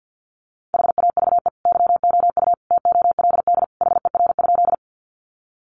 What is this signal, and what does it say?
Morse code, heard as a keyed tone, about 50 wpm, 700 Hz. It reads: HAVE YOU TOLD HER?
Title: Sentences from Top 400 Words 50wpm